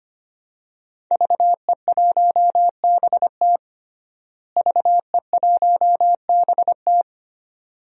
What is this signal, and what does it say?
Morse code, heard as a keyed tone, about 25 wpm, 700 Hz. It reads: VE1BT VE1BT
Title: DXCC-100 Call Signs - 5 Characters - Repeated 2x - 25wpm